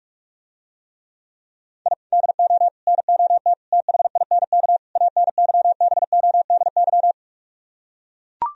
Morse code, heard as a keyed tone, about 44 wpm, 700 Hz. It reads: I DO NOT THINK ANYBODY